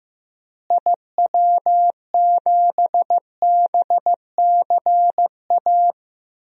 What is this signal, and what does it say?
Morse code, heard as a keyed tone, about 15 wpm, 700 Hz. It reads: IW7BCA
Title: DXCC-100 Call Signs - 6 Characters - Repeated 2x - 15wpm